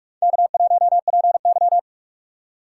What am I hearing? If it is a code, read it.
Morse code, heard as a keyed tone, about 45 wpm, 700 Hz. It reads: K1PY
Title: USA Call Signs - 2x Character Spacing - 45wpm